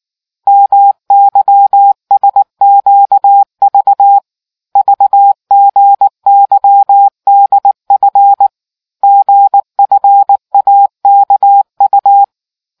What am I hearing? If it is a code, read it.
Morse code, heard as a keyed tone, about 19 wpm, 800 Hz. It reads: MYSQV VGYDF GFAKU